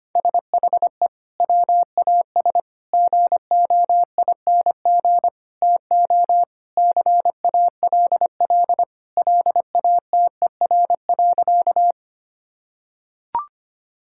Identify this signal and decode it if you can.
Morse code, heard as a keyed tone, about 25 wpm, 700 Hz. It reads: SHE WAS GOING TO CALL LATER.
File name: practice / 25wpm / Sentences with 4-Word Phrases - 25wpm